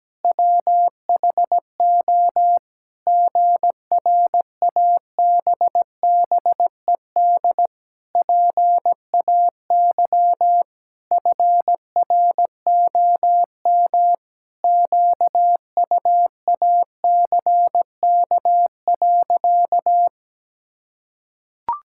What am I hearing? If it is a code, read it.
Morse code, heard as a keyed tone, about 17 wpm, 700 Hz. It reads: WHO GRABBED PAY FROM QUACK.
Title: Pangram Challenge v2 - 17wpm